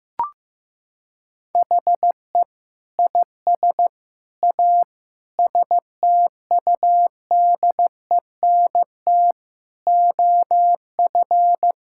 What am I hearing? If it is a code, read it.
Morse code, heard as a keyed tone, about 15 wpm, 700 Hz. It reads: HE IS A STUDENT OF
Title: Sentences from Top 1000 Words - Review Part 6 - 15wpm